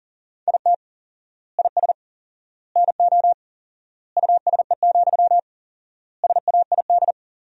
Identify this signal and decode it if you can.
Morse code, heard as a keyed tone, about 37 wpm, 700 Hz. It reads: IT IS NO USE, SAID
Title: 4x Word-Spacing - Sentences from Top 500 Words - 40wpm